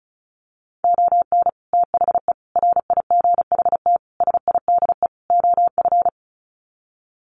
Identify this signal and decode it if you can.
Morse code, heard as a keyed tone, about 34 wpm, 700 Hz. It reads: ON THE RIGHT SIDE OF
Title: Sets of 5 Words Encore 35wpm